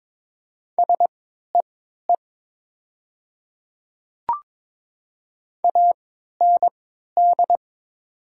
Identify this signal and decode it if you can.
Morse code, heard as a keyed tone, about 22 wpm, spaced as for 12 wpm, 700 Hz. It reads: SEE AND
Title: 3x Character Spacing - Top 200 Words - 22wpm